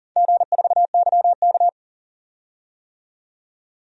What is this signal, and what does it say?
Morse code, heard as a keyed tone, about 40 wpm, 700 Hz. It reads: G4YK
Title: DXCC-100 Call Signs - 4 Characters - 40wpm